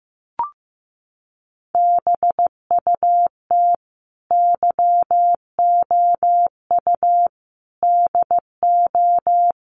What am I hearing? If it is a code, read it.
Morse code, heard as a keyed tone, about 14 wpm, 700 Hz. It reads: BUT YOU DO
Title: Sentences from Top 600 Words 15wpm